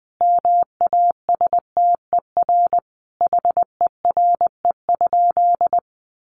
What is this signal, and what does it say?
Morse code, heard as a keyed tone, about 19 wpm, 700 Hz. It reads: MASTER HERE?
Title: Sentences from Top 1300 Words 20wpm